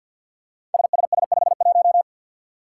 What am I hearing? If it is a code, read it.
Morse code, heard as a keyed tone, about 48 wpm, 700 Hz. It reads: SSS51
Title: Five Challenging Letter-Number Combinations 50wpm